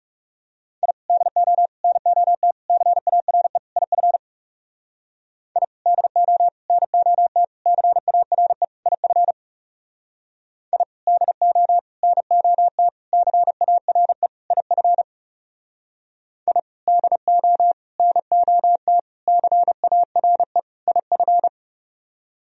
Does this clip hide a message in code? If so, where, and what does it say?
Yes: Morse code from 0.8 to 21.5 s, heard as a keyed tone, about 39 wpm, 700 Hz. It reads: I DO NOT CARE IF I DO NOT CARE IF I DO NOT CARE IF I DO NOT CARE IF